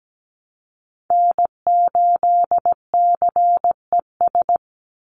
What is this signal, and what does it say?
Morse code, heard as a keyed tone, about 17 wpm, 700 Hz. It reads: N8CES
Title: USA Call Signs 17wpm